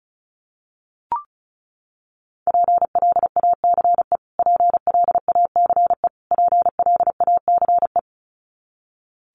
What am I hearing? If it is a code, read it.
Morse code, heard as a keyed tone, about 34 wpm, 700 Hz. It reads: PLACE PLACE PLACE
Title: Top 500 Words Repeated 35wpm